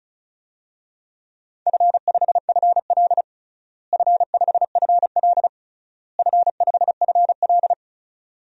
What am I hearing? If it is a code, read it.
Morse code, heard as a keyed tone, about 36 wpm, 700 Hz. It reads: F5FL F5FL F5FL
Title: DXCC-100 Call Signs - 4 Characters - Repeated 3x - 35wpm